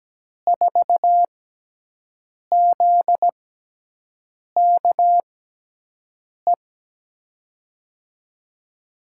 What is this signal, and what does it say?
Morse code, heard as a keyed tone, about 17 wpm, spaced as for 6 wpm, 700 Hz. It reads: 4ZKE
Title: USA Call Signs - 5 Characters - 6x Character Spacing - 17wpm